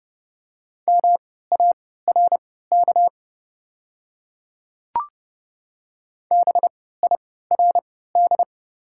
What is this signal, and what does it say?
Morse code, heard as a keyed tone, about 29 wpm, spaced as for 17 wpm, 700 Hz. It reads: MARK BIRD
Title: 3x Character Spacing - Top 500 Words - 30wpm